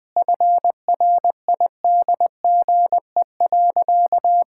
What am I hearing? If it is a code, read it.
Morse code, heard as a keyed tone, about 20 wpm, 700 Hz. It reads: FRIDGE.
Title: Easy Sentences with Hints - 20wpm